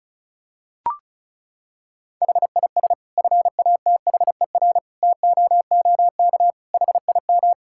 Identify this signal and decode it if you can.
Morse code, heard as a keyed tone, about 36 wpm, 700 Hz. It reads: HIS FATHER TOOK HIM
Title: Sentences from Top 1000 Words - Review Part 5 - 35wpm